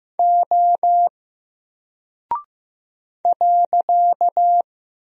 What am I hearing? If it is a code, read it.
Morse code, heard as a keyed tone, about 15 wpm, 700 Hz. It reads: O .
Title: Single Character (Letter:Number:Punctuation:Prosign) Alt 15wpm